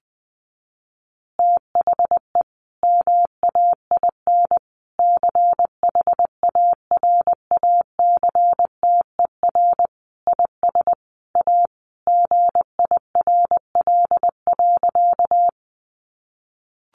Morse code, heard as a keyed tone, about 20 wpm, 700 Hz. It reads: THE MAIN CHARACTER IS A GIRL.
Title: Additional Easy Phrases 20wpm